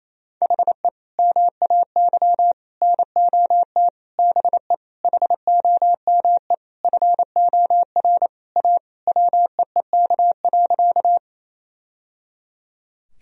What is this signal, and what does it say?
Morse code, heard as a keyed tone, about 28 wpm, 700 Hz. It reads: HE MAY NOT BE HOME FOR A WEEK.
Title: Sentences from Top 1000 Words - Review Part 2 - 28wpm